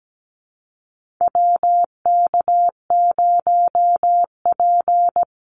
Morse code, heard as a keyed tone, about 17 wpm, 700 Hz. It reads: WK0P
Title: Call Signs - 17wpm